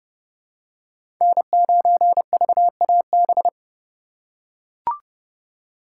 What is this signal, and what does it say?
Morse code, heard as a keyed tone, about 30 wpm, 700 Hz. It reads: N9VAB
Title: USA Call Signs - 4x Character Spacing - 30wpm